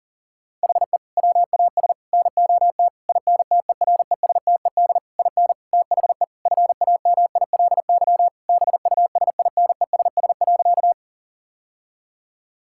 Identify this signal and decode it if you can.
Morse code, heard as a keyed tone, about 40 wpm, 700 Hz. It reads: HE WAS NOT INTERESTED IN THE FAMILY BUSINESS.
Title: Sentences from Top 1000 Words - Review Part 6 - 40wpm